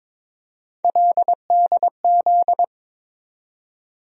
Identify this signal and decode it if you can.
Morse code, heard as a keyed tone, about 22 wpm, 700 Hz. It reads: LDZ
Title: USA Call Sign Postfixes - 3 Letters 22wpm